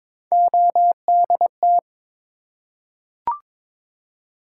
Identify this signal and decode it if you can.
Morse code, heard as a keyed tone, about 22 wpm, 700 Hz. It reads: ODT
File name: practice / 22wpm / Three Letter-Number Combinations - 22wpm